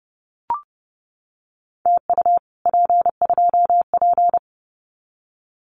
Morse code, heard as a keyed tone, about 31 wpm, 700 Hz. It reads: TU P2P
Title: Common QSO Phrases Encore 30wpm